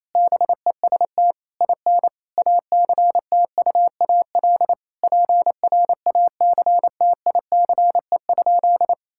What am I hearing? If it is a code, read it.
Morse code, heard as a keyed tone, about 27 wpm, 700 Hz. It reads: BEST IN ACTUAL PRACTICE?
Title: Sentences from Top 1000 Words - Review Part 5 - 28wpm